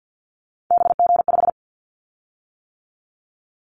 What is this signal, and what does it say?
Morse code, heard as a keyed tone, about 50 wpm, 700 Hz. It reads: BG5
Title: Callsign Prefixes for Top 100 DXCC - 50wpm